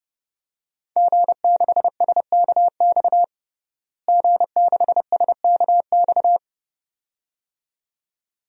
Morse code, heard as a keyed tone, about 30 wpm, 700 Hz. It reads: G6SKX G6SKX
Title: DXCC-100 Call Signs - 5 Characters - Repeated 2x - 30wpm